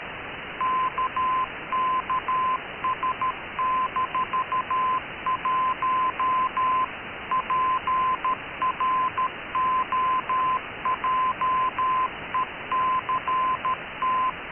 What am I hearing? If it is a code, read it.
Morse code, heard as a keyed tone, about 13 wpm, 1030 Hz. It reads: KKS-1 PROJECT
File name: KKS-1 cw 21jul2014 1425z